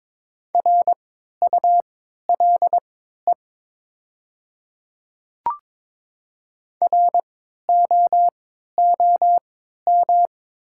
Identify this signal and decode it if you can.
Morse code, heard as a keyed tone, about 22 wpm, spaced as for 12 wpm, 700 Hz. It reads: RULE ROOM
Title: 3x Character Spacing - Top 500 Words - 22wpm